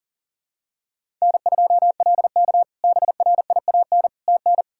Morse code, heard as a keyed tone, about 37 wpm, 700 Hz. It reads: N2LK BRIAN TN
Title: K1USN SST QSO Practice - 40wpm